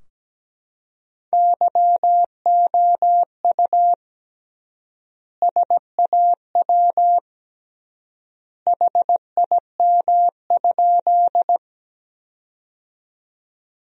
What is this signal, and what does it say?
Morse code, heard as a keyed tone, about 17 wpm, 700 Hz. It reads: YOU SAW HIM?
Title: 3x Word-Spacing - Sentences from Top 300 Words - 17wpm